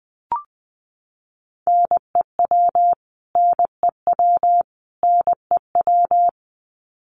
Morse code, heard as a keyed tone, about 20 wpm, 700 Hz. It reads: NEW NEW NEW
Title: Top 200 Words Repeated 20wpm